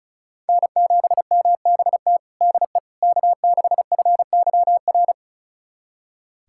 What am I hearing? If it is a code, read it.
Morse code, heard as a keyed tone, about 36 wpm, 700 Hz. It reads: N7MBT DE K6FYR
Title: USA Callsign DE Callsign - 5 Characters - 35wpm